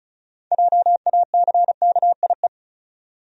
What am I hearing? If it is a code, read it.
Morse code, heard as a keyed tone, about 34 wpm, 700 Hz. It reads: JACKIE
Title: USA Common Names - 35wpm